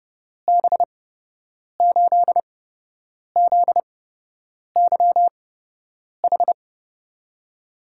Morse code, heard as a keyed tone, about 30 wpm, spaced as for 8 wpm, 700 Hz. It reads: B8ZYH
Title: USA Call Signs - 6 Characters - 8x Character Spacing - 30wpm